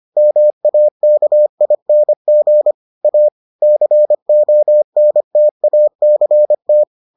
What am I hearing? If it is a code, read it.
Morse code, heard as a keyed tone, about 25 wpm, 600 Hz. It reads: MAKING A CONTACT